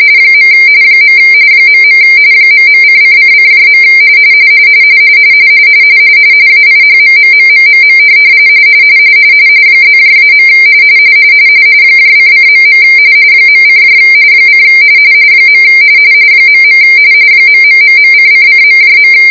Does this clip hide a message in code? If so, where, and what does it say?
Yes: RTTY from 0.3 to 19.1 s, about 45.45 baud. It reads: . .. ..::::. AIHMMMMMM +-. ::.. .: : . .. ....:::..